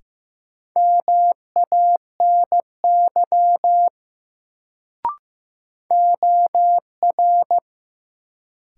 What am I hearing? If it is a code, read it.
Morse code, heard as a keyed tone, about 15 wpm, 700 Hz. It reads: MANY OR
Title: Sentences from Top 100 Words - New Word Drill - 15wpm